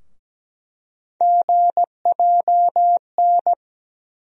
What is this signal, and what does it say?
Morse code, heard as a keyed tone, about 17 wpm, 700 Hz. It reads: GJN
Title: USA Call Sign Postfixes - 3 Letters 17wpm